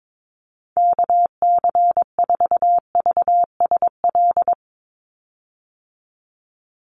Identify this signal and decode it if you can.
Morse code, heard as a keyed tone, about 22 wpm, 700 Hz. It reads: KC4VSL
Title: USA Call Signs 22wpm